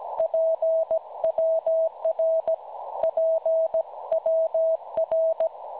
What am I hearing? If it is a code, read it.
Morse code, heard as a keyed tone, about 17 wpm, 650 Hz. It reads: PWR PWR
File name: cwronde / websdr recording .6kHz